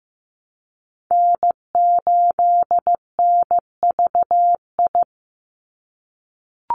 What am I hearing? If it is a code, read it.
Morse code, heard as a keyed tone, about 15 wpm, 700 Hz. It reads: N8NVI
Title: USA Call Signs 15wpm